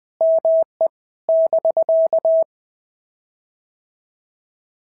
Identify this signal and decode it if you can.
Morse code, heard as a keyed tone, about 18 wpm, 650 Hz. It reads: ME <BK>
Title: POTA QSO Practice - 20wpm